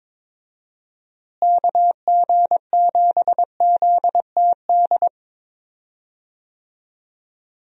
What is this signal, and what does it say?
Morse code, heard as a keyed tone, about 22 wpm, 700 Hz. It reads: KG7ZTD
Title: USA Call Signs - 6 Characters - Repeated 3x - 22wpm